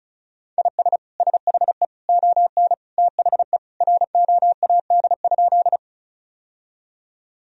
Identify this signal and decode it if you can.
Morse code, heard as a keyed tone, about 36 wpm, 700 Hz. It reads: IS SHE ON THE ROAD?
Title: Sentences from Top 1000 Words - Review Part 2 - 35wpm